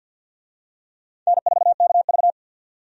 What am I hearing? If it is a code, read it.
Morse code, heard as a keyed tone, about 50 wpm, 700 Hz. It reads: N4KV
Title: USA Call Signs - 4 Characters - 50wpm